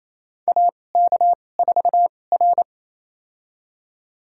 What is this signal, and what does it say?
Morse code, heard as a keyed tone, about 28 wpm, spaced as for 20 wpm, 700 Hz. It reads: AK4R
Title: USA Call Signs - 4 Characters - 2x Character Spacing - 28wpm